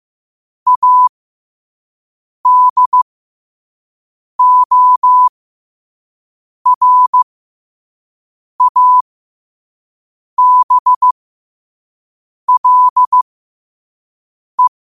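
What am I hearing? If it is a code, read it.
Morse code, heard as a keyed tone, about 15 wpm, spaced as for 5 wpm, 1000 Hz. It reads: ADORABLE
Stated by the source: Morse code sent at 5 wpm.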